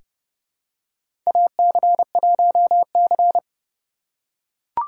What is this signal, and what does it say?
Morse code, heard as a keyed tone, about 31 wpm, 700 Hz. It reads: AC1C
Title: USA Call Signs - 4 Characters - 2x Character Spacing - 30wpm